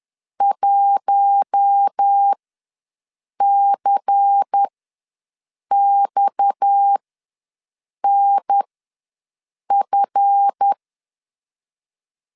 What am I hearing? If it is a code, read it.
Morse code, heard as a keyed tone, about 10 wpm, spaced as for 6 wpm, 800 Hz. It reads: 1CXNF
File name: LEZIONI / 7 GF9Z / 7 GF ws 10wpm 3918